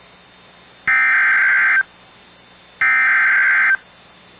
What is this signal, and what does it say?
Emergency Alert System (SAME) tones (header bursts), heard as harsh buzzing bursts. It is heard from 0.9 to 3.7 s.